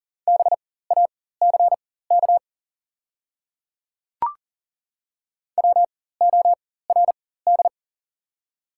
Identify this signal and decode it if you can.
Morse code, heard as a keyed tone, about 40 wpm, spaced as for 19 wpm, 700 Hz. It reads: BACK WORD
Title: 4x Character Spacing - Top 200 Words - 40wpm